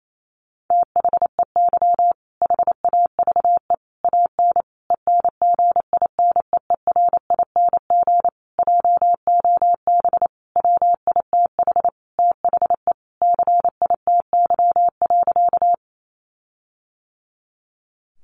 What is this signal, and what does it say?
Morse code, heard as a keyed tone, about 28 wpm, 700 Hz. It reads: THEY HAVE AN ENGINEERING JOB WITH THE CITY.
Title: Sentences from Top 1500 Words 28wpm